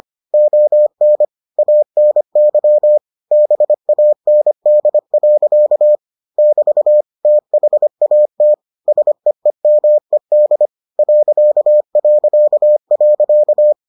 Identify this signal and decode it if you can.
Morse code, heard as a keyed tone, about 25 wpm, 600 Hz. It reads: ON ANY BAND. <BT> THAT SEEMED ...